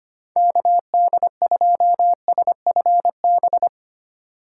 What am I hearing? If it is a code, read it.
Morse code, heard as a keyed tone, about 25 wpm, 700 Hz. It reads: KD2SFB
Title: USA Call Signs - 6 Characters - 4x Character Spacing - 25wpm